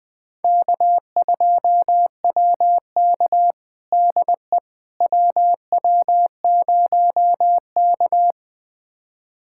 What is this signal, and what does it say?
Morse code, heard as a keyed tone, about 21 wpm, 700 Hz. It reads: K2WK DE WW0K
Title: USA Callsign DE Callsign - 4 Characters - 20wpm